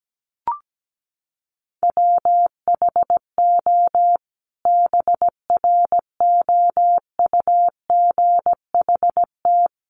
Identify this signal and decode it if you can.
Morse code, heard as a keyed tone, about 17 wpm, 700 Hz. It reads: WHO BROUGHT
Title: Sentences from Top 500 Words 17wpm